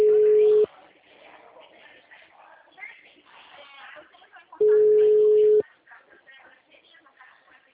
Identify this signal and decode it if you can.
Ringback tone (Europe, 425 Hz), with speech.
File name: out- -2005- - - .53117